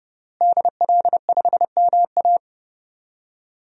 Morse code, heard as a keyed tone, about 32 wpm, 700 Hz. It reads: DL5MA
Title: DXCC-100 Call Signs - 5 Characters - Repeated 3x - 30wpm